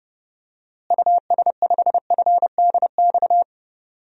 Morse code, heard as a keyed tone, about 30 wpm, 700 Hz. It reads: US5FDX
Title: DXCC-100 Call Signs - 6 Characters - Repeated 2x - 30wpm